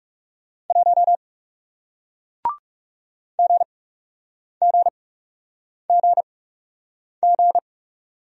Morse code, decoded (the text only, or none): 1 GGGG